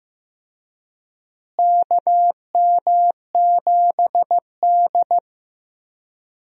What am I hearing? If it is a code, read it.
Morse code, heard as a keyed tone, about 15 wpm, 700 Hz. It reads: KM7D
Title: USA Call Signs - 4 Characters - Repeated 2x - 15wpm